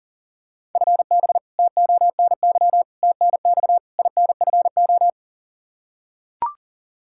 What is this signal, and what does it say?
Morse code, heard as a keyed tone, about 39 wpm, 690 Hz. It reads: FB TONY TNX INFO
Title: Practice QSO - Exchange 3 - More Info - 40wpm